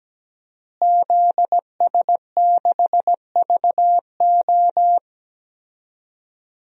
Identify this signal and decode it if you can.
Morse code, heard as a keyed tone, about 17 wpm, 700 Hz. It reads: ZS6VO
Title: DXCC-100 Call Signs - 5 Characters - Repeated 2x - 17wpm